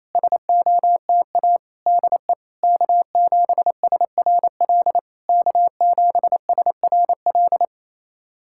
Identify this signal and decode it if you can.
Morse code, heard as a keyed tone, about 28 wpm, 660 Hz. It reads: SOTA DE K7SRL K7SRL
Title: SOTA QSO Practice 28wpm